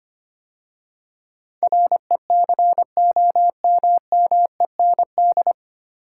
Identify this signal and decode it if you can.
Morse code, heard as a keyed tone, about 25 wpm, 700 Hz. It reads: RECOMMEND
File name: practice / 25wpm / 9 Letter Words - First 3 Letters Given - 25wpm